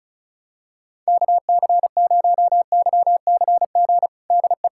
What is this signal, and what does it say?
Morse code, heard as a keyed tone, about 34 wpm, 700 Hz. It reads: KC0YCG DE
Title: USA Callsign DE Callsign - 6 Characters - 35wpm